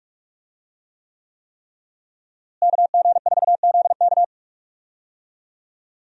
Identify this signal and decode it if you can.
Morse code, heard as a keyed tone, about 46 wpm, 700 Hz. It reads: KG4ZK
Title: USA Call Signs - 5 Characters - Repeated 3x - 45wpm